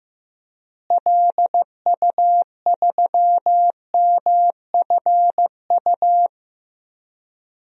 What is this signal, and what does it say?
Morse code, heard as a keyed tone, about 15 wpm, 700 Hz. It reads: LU3MFU
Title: DXCC-100 Call Signs - 6 Characters - Repeated 2x - 15wpm